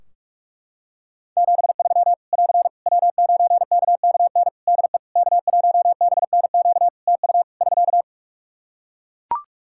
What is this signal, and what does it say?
Morse code, heard as a keyed tone, about 41 wpm, 700 Hz. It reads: 73 <AR> W9KKN DE K1BNQ TU <SK>
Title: K1USN SST QSO Practice 45wpm